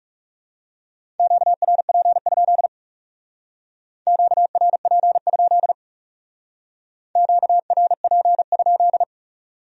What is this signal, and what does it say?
Morse code, heard as a keyed tone, about 36 wpm, 700 Hz. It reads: QRP? QRP? QRP?
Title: Speed-Racing Q-Codes - 30wpm